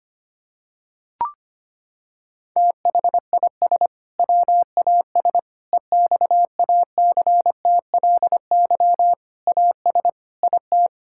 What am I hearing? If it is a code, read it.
Morse code, heard as a keyed tone, about 25 wpm, 700 Hz. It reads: THIS WAS EXACTLY AS IT
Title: Sentences from Top 1000 Words - Review Part 6 - 25wpm